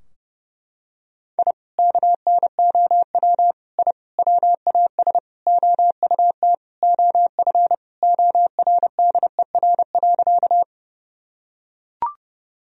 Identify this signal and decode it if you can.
Morse code, heard as a keyed tone, about 30 wpm, 700 Hz. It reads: I KNOW I WAS OUT OF ORDER.
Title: Sentences from Top 1000 Words - Review Part 2 - 30wpm